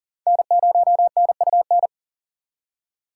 Morse code, heard as a keyed tone, about 53 wpm, 700 Hz. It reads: N0NUN